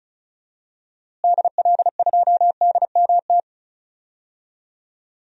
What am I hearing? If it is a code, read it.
Morse code, heard as a keyed tone, about 36 wpm, 700 Hz. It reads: DL2DMT
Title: DXCC-100 Call Signs - 6 Characters - Repeated 2x - 35wpm